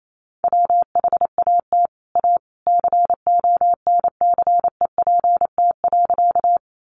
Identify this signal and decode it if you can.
Morse code, heard as a keyed tone, about 28 wpm, 700 Hz. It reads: WHAT A CONCEPT.
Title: Sentences from Top 1100 Words 28wpm